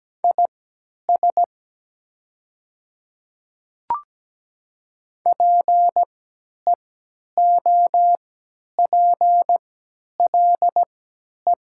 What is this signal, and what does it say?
Morse code, heard as a keyed tone, about 17 wpm, spaced as for 10 wpm, 690 Hz. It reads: IS PEOPLE
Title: 3x Character Spacing - Top 100 Words - 17wpm